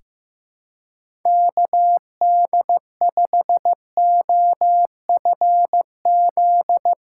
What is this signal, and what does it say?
Morse code, heard as a keyed tone, about 15 wpm, 700 Hz. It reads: KD5OFZ
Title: USA Callsign DE Callsign - 6 Characters - 15wpm